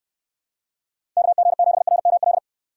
Morse code, heard as a keyed tone, about 68 wpm, 700 Hz. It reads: KG7RML